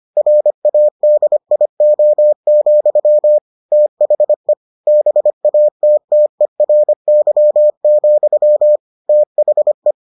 Morse code, heard as a keyed tone, about 25 wpm, 600 Hz. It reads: RADIO, THE BATTERY, THE